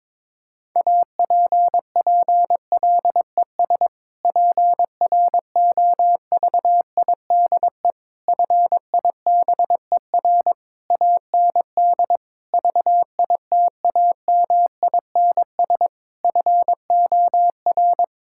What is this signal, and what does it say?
Morse code, heard as a keyed tone, about 22 wpm, 700 Hz. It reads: APPLES PROVIDE FIBER AND VITAMINS FOR
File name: Difficult Sentences with Hints - 22wpm